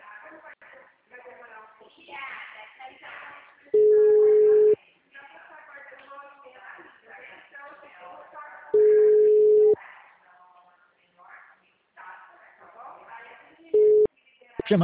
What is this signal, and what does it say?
Ringback tone (Europe, 425 Hz), with speech.